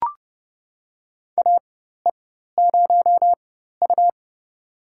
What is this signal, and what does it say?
Morse code, heard as a keyed tone, about 30 wpm, spaced as for 14 wpm, 700 Hz. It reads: AE0U
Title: USA Call Signs - 4 Characters - 4x Character Spacing - 30wpm